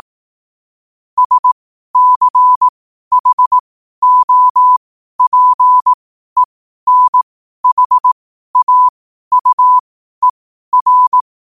Morse code, heard as a keyed tone, about 18 wpm, spaced as for 12 wpm, 1000 Hz. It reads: SCHOPENHAUER
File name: qotdfiles / 12wpm